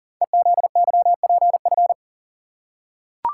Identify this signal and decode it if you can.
Morse code, heard as a keyed tone, about 42 wpm, 700 Hz. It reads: EZYPF
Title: Code Groups - 5 Letters - 40wpm